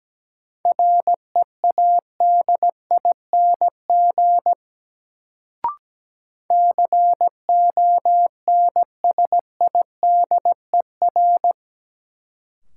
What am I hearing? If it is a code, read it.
Morse code, heard as a keyed tone, about 17 wpm, 700 Hz. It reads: READING CONSIDER
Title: Sentences from Top 600 Words - New Word Drill - 17wpm